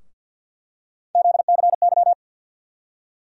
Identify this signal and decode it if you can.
Morse code, heard as a keyed tone, about 48 wpm, 700 Hz. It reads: ZCY
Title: USA Call Sign Postfixes - 3 Letters 50wpm